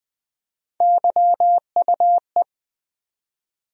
Morse code, heard as a keyed tone, about 20 wpm, 700 Hz. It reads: YUE
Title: USA Call Sign Postfixes - 3 Letters 20wpm